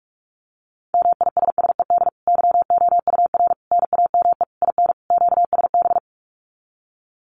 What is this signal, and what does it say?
Morse code, heard as a keyed tone, about 44 wpm, 700 Hz. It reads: MISSED YOUR NAME IN QSB
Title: Common QSO Phrases 45wpm